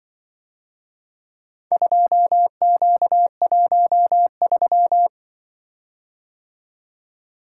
Morse code, heard as a keyed tone, about 25 wpm, 700 Hz. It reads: 2Q13
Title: Speed-Racing - Four Letter-Number Combinations - 20wpm